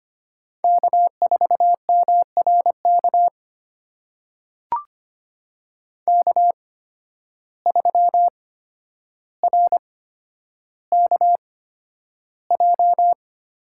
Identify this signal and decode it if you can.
Morse code, heard as a keyed tone, about 25 wpm, spaced as for 11 wpm, 700 Hz. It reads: K4MRK K3RKJ
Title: USA Call Signs - 8x Character Spacing - 25wpm